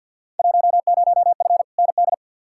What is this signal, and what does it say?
Morse code, heard as a keyed tone, about 46 wpm, 700 Hz. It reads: 10F ND
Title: Field Day Exchanges Encore - 50wpm